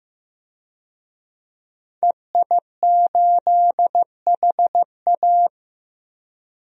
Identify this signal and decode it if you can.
Morse code, heard as a keyed tone, about 15 wpm, 700 Hz. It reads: EI8HA
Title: DXCC-100 Call Signs - 5 Characters - Repeated 3x - 15wpm